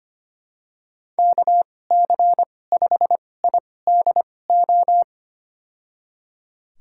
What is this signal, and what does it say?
Morse code, heard as a keyed tone, about 25 wpm, spaced as for 18 wpm, 700 Hz. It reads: KC5IDO
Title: USA Call Signs - 6 Characters - 2x Character Spacing - 25wpm